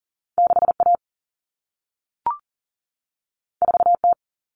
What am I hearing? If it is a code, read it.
Morse code, heard as a keyed tone, about 40 wpm, 700 Hz. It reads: 6A 4T
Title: Common Cut Numbers - 2-Digit 40wpm